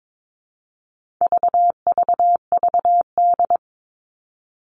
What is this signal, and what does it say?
Morse code, heard as a keyed tone, about 22 wpm, 700 Hz. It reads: VVVD
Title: All Cut Numbers - 4-Digit 22wpm